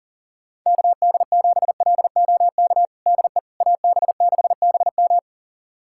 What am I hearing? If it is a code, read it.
Morse code, heard as a keyed tone, about 40 wpm, 700 Hz. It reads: KD7LOK DE AB6BM
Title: USA Callsign DE Callsign 40wpm